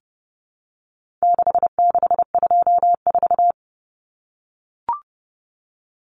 Morse code, heard as a keyed tone, about 28 wpm, 700 Hz. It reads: 6624
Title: Numbers - 4-Digit 30wpm